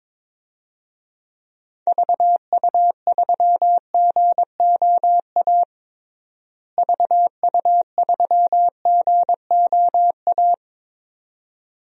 Morse code, heard as a keyed tone, about 22 wpm, 700 Hz. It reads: VU3GOA VU3GOA
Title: DXCC-100 Call Signs - 6 Characters - Repeated 2x - 22wpm